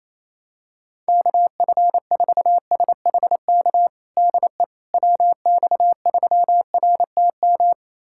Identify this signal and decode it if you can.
Morse code, heard as a keyed tone, about 28 wpm, 700 Hz. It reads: KF4SHK DE WX3RTM
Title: USA Callsign DE Callsign - 6 Characters - 28wpm